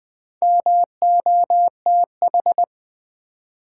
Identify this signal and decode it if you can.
Morse code, heard as a keyed tone, about 20 wpm, 700 Hz. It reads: MOTH
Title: Build A Word - 2 Letters At A Time - 6-Letter Words - 20wpm